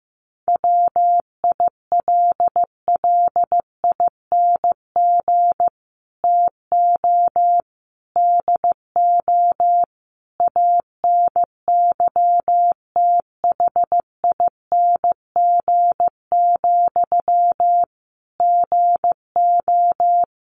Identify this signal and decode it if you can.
Morse code, heard as a keyed tone, about 14 wpm, 700 Hz. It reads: WILLING TO DO ANYTHING, GO
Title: Sentences from Top 1400 Words 15wpm